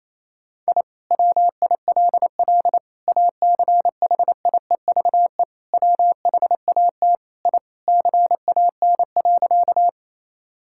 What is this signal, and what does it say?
Morse code, heard as a keyed tone, about 28 wpm, 700 Hz. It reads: I WILL ACHIEVE WHAT I CAN.
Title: Sentences from Top 1000 Words - Review Part 6 - 28wpm